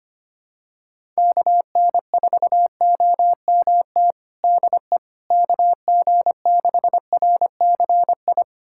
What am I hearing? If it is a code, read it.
Morse code, heard as a keyed tone, about 25 wpm, 700 Hz. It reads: KN4OMT DE KG6RCI
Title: USA Callsign DE Callsign - 6 Characters - 25wpm